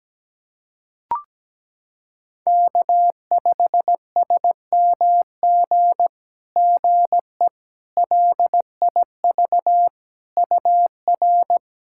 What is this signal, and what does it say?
Morse code, heard as a keyed tone, about 17 wpm, 700 Hz. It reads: K5SMG GE LIV UR
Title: SOTA QSO Practice 17wpm